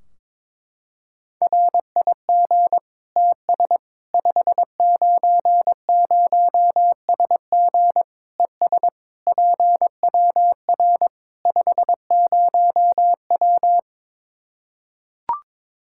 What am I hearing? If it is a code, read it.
Morse code, heard as a keyed tone, about 22 wpm, 700 Hz. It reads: RIG TS 590SG ES PWR 50W
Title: Practice QSO - Exchange 2 - Info - 22wpm